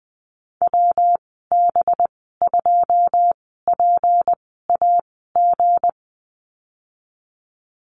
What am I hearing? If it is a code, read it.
Morse code, heard as a keyed tone, about 20 wpm, spaced as for 15 wpm, 700 Hz. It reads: WB2PAG
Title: 186 Lesson 30g - Letter b - Call Signs 20wpm